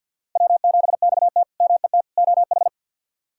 Morse code, heard as a keyed tone, about 47 wpm, 700 Hz. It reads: W7XT MET OH
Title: K1USN SST QSO Practice - 50wpm